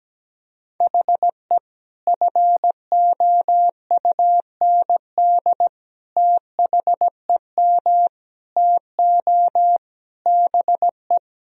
Morse code, heard as a keyed tone, about 17 wpm, 700 Hz. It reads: HE FOUND THEM TO BE
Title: Sentences from Top 1000 Words - Review Part 3 - 17wpm